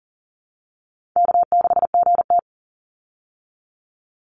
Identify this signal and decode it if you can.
Morse code, heard as a keyed tone, about 40 wpm, 700 Hz. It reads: K6GT
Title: Call Signs Repeated 3x - 40wpm